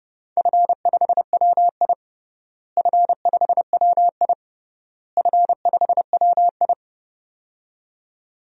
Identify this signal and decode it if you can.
Morse code, heard as a keyed tone, about 32 wpm, 700 Hz. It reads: F5WI F5WI F5WI
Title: DXCC-100 Call Signs - 4 Characters - Repeated 3x - 30wpm